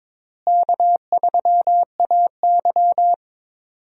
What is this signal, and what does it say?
Morse code, heard as a keyed tone, about 22 wpm, 700 Hz. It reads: K3AY
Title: USA Call Signs - 4 Characters - Repeated 2x - 22wpm